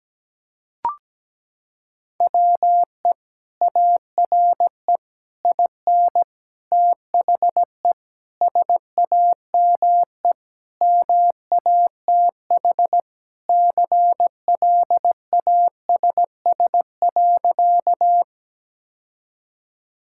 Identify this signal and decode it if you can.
Morse code, heard as a keyed tone, about 17 wpm, 700 Hz. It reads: WE ARE IN THE SAME MATH CLASS.
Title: Sentences from Top 1000 Words - Review Part 6 - 17wpm